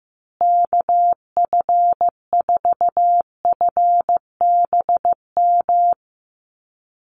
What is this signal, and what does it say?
Morse code, heard as a keyed tone, about 14 wpm, 700 Hz. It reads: KF4FBM
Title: USA Call Signs Repeated 2x 15wpm